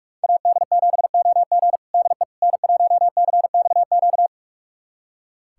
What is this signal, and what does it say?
Morse code, heard as a keyed tone, about 46 wpm, 700 Hz. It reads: AD7OG DE N1CXQ
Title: USA Callsign DE Callsign - 5 Characters - 45wpm